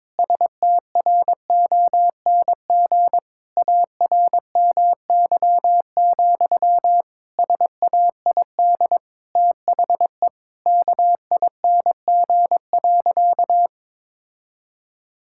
Morse code, heard as a keyed tone, about 21 wpm, 700 Hz. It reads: STRONG ARMY, SAID THE KING.
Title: Sentences from Top 1000 Words - Review Part 5 - 22wpm